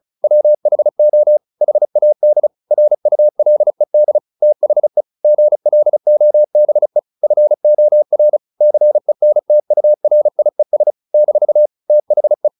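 Morse code, heard as a keyed tone, about 34 wpm, 600 Hz. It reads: WHO HAD RULED THE GLOBE FOR CENTURIES - THE
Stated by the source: Morse practice at 35 wpm from CW med Gnister, sending Ken Follett's novel Never in English.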